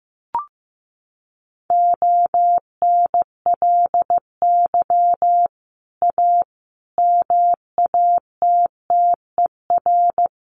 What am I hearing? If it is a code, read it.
Morse code, heard as a keyed tone, about 15 wpm, 700 Hz. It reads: ONLY A MATTER
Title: Sets of 5 Words Encore 15wpm